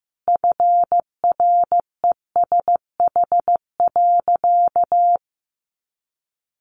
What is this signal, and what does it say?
Morse code, heard as a keyed tone, about 14 wpm, 700 Hz. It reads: FRESH.
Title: Sentences from Top 1200 Words 15wpm